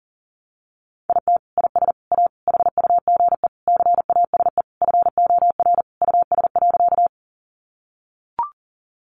Morse code, heard as a keyed tone, about 40 wpm, 700 Hz. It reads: IT IS A HUGE CASE FOR US.
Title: Sentences from Top 1100 Words 40wpm